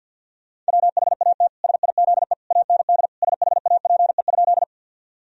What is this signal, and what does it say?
Morse code, heard as a keyed tone, about 50 wpm, 700 Hz. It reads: WHAT SIZE AND SHAPE?
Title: Sentences from Top 1000 Words - Review Part 3 - 50wpm